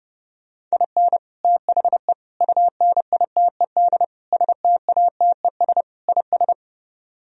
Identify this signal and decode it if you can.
Morse code, heard as a keyed tone, about 37 wpm, 700 Hz. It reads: IN THE UNITED STATES IS
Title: Speed-Racing - Sets of 5 Words 30wpm